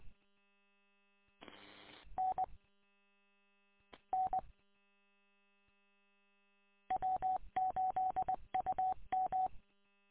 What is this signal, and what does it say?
Morse code, heard as a keyed tone, about 20 wpm, 750 Hz. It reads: N N W8UM
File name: w8um- -combined